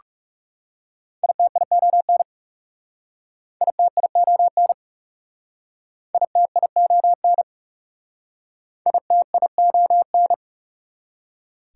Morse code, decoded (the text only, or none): ITION ITION ITION ITION